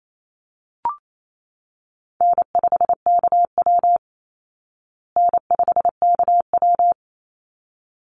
Morse code, heard as a keyed tone, about 28 wpm, 700 Hz. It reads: N5KW N5KW
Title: Easy Call Signs Repeated 2x - 28wpm